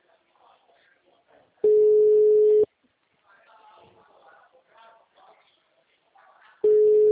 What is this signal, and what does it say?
Ringback tone (Europe, 425 Hz).